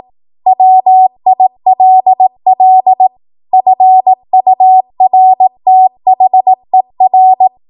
Morse code, heard as a keyed tone, about 18 wpm, 750 Hz. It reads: WILL FURTHER
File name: Morse / 18WPM